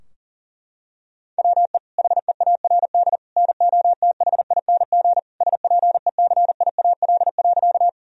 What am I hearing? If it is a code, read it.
Morse code, heard as a keyed tone, about 40 wpm, 700 Hz. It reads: WE HEARD NOTHING SPECIAL.
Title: Sentences from Top 1000 Words - Review Part 3 - 40wpm